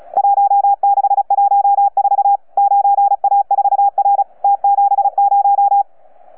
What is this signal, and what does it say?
Morse code, heard as a keyed tone, about 35 wpm, 800 Hz. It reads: 1614 9A4R T70
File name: UP2L-20-CW